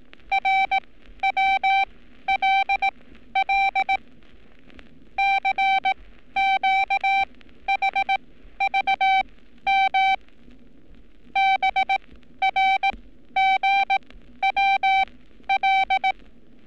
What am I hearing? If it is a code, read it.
Morse code, heard as a keyed tone, about 18 wpm, spaced as for 12 wpm, 780 Hz. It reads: RWLL CQHVM BRGWL
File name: training / NAVPERS 11224-11235 / NAVPERS 11227 RA RB